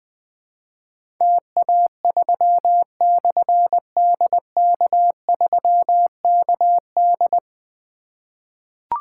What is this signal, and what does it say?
Morse code, heard as a keyed tone, about 20 wpm, 700 Hz. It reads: TA3/DK3KD
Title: Difficult Call Signs 20wpm